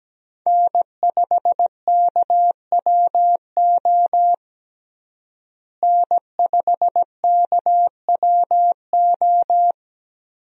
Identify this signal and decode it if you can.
Morse code, heard as a keyed tone, about 17 wpm, 700 Hz. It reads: N5KWO N5KWO
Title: USA Call Signs - 5 Characters - Repeated 3x - 17wpm